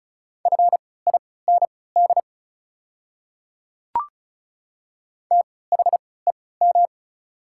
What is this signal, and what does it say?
Morse code, heard as a keyed tone, about 36 wpm, spaced as for 20 wpm, 690 Hz. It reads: FIND THEM
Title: 3x Character Spacing - Top 100 Words - 35wpm